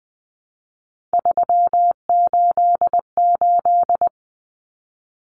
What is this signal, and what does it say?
Morse code, heard as a keyed tone, about 20 wpm, 700 Hz. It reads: 388
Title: Common Cut Numbers - 3-Digit 20wpm